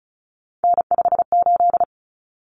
Morse code, heard as a keyed tone, about 34 wpm, 700 Hz. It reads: N58
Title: Common Cut Numbers - 3-Digit 35wpm